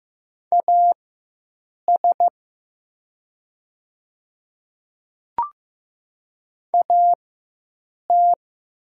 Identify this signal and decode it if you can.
Morse code, heard as a keyed tone, about 15 wpm, spaced as for 7 wpm, 700 Hz. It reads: AS AT
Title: 4x Character Spacing - Top 100 Words - 15wpm